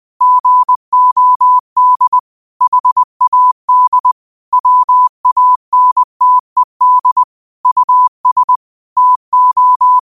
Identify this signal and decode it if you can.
Morse code, heard as a keyed tone, about 20 wpm, 1000 Hz. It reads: GOD HAD WANTED US TO
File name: qotdfiles / 20wpm